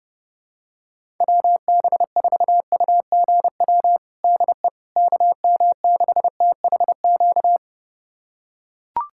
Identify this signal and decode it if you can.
Morse code, heard as a keyed tone, about 32 wpm, 700 Hz. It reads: WB4UGW DE KM6THQ
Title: USA Callsign DE Callsign - 6 Characters - 30wpm